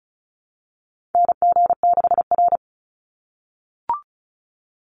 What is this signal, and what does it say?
Morse code, heard as a keyed tone, about 34 wpm, 700 Hz. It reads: NG6R
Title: NA SOTA Chaser Callsigns 35wpm